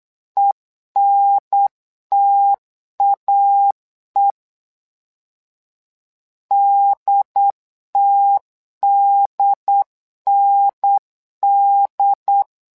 Morse code, heard as a keyed tone, about 8 wpm, 800 Hz. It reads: ENTAE DTDND
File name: LEZIONI / 2 UHDN / 2 UHDN 8wpm2232